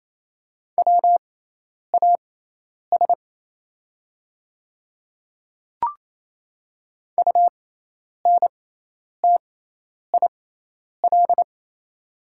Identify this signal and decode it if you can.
Morse code, heard as a keyed tone, about 28 wpm, spaced as for 10 wpm, 700 Hz. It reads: WAS UNTIL
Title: 6x Character Spacing - Top 300 Words - 28wpm